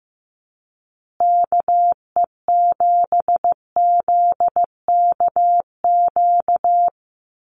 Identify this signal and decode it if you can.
Morse code, heard as a keyed tone, about 15 wpm, 700 Hz. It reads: KE7ZKQ
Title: USA Call Signs Repeated 2x 15wpm